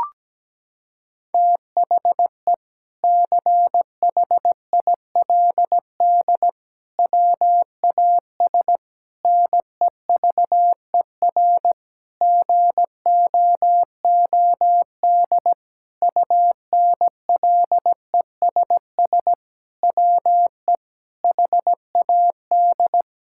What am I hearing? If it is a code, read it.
Morse code, heard as a keyed tone, about 17 wpm, 700 Hz. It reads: THE CHILD WAS NEVER GOOD UNLESS WE HAD
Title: Sentences from Top 1000 Words - Review Part 6 - 17wpm